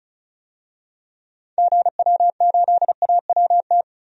Morse code, heard as a keyed tone, about 36 wpm, 700 Hz. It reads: GW8AWT
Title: DXCC-100 Call Signs - 6 Characters - 35wpm